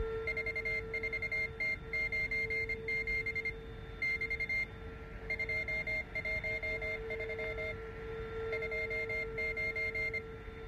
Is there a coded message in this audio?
Yes: Morse code from 0.3 to 10.2 s, heard as a keyed tone, about 25 wpm, 2100 Hz. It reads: 44T97 <BT> 213 29